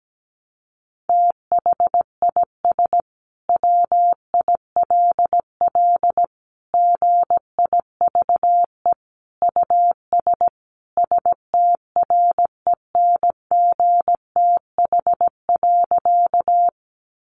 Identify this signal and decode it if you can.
Morse code, heard as a keyed tone, about 17 wpm, 700 Hz. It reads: THIS WILL GIVE US STRENGTH.
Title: Sentences from Top 1300 Words 17wpm